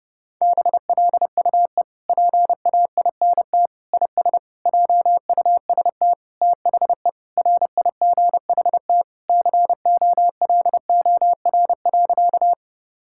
Morse code, heard as a keyed tone, about 30 wpm, 700 Hz. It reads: BLUE PAINT IS JUST THE RIGHT COLOR.
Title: Sentences from Top 1000 Words - Review Part 6 - 30wpm